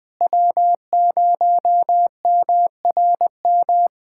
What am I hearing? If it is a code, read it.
Morse code, heard as a keyed tone, about 20 wpm, 700 Hz. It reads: W0MRM
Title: USA Callsign DE Callsign - 5 Characters - 20wpm